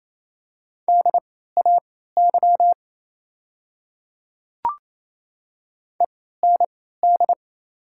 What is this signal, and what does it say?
Morse code, heard as a keyed tone, about 27 wpm, spaced as for 16 wpm, 690 Hz. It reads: DAY END
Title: 3x Character Spacing - Top 300 Words - 28wpm